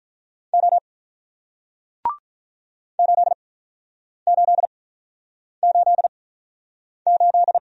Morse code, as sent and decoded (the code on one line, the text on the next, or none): --- / ---.. ---.. ---.. ---..
O 8888